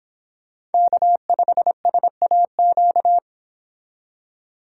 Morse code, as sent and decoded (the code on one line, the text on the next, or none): -.- ..... ... .- --.-
K5SAQ